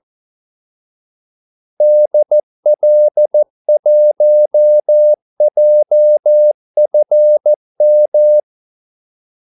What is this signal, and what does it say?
Morse code, heard as a keyed tone, about 14 wpm, 600 Hz. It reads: DL1JFM
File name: Callsigns-14wpm